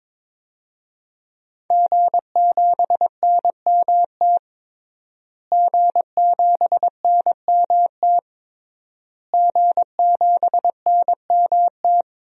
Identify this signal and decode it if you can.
Morse code, heard as a keyed tone, about 22 wpm, 700 Hz. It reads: G7NMT G7NMT G7NMT
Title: DXCC-100 Call Signs - 5 Characters - Repeated 3x - 22wpm